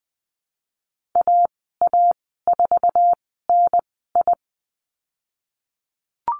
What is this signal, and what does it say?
Morse code, heard as a keyed tone, about 20 wpm, spaced as for 15 wpm, 700 Hz. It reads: AA4NI
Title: 076 Lesson 13g - Letter L - Call Signs 20wpm